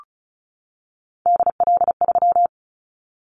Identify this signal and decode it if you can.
Morse code, heard as a keyed tone, about 36 wpm, 700 Hz. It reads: DL3
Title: Callsign Prefixes for Top 50 DXCC - 35wpm